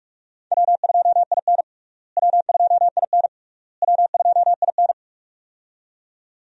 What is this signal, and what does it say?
Morse code, heard as a keyed tone, about 41 wpm, 700 Hz. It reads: W2IN W2IN W2IN
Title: USA Call Signs - 4 Characters - Repeated 3x - 45wpm